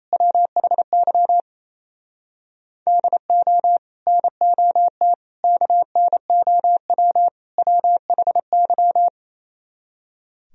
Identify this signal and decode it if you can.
Morse code, heard as a keyed tone, about 37 wpm, 700 Hz. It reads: WHY DO NOT KNOW WHY